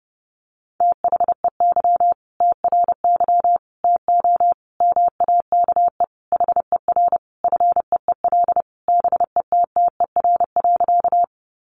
Morse code, heard as a keyed tone, about 30 wpm, 700 Hz. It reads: THEY TRY TO MAKE HER FEEL BETTER.
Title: Easy Phrases 30wpm